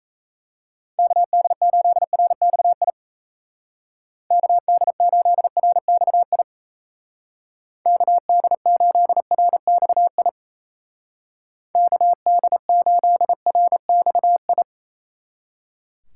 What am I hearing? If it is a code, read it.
Morse code, heard as a keyed tone, about 34 wpm, 700 Hz. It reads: KD8RXI KD8RXI KD8RXI KD8RXI
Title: Speed Racing - USA 2x3 Call Signs 28wpm